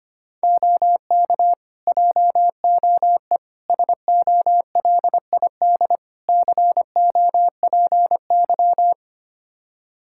Morse code, heard as a keyed tone, about 25 wpm, 700 Hz. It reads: OK JOE SOLID COPY
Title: Practice QSO - Exchange 3 - More Info - 25wpm